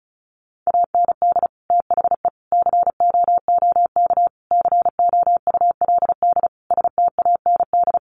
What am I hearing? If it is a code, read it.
Morse code, heard as a keyed tone, about 34 wpm, 700 Hz. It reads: AND THE COOK COULD STAND
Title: Sentences from Top 1300 Words 35wpm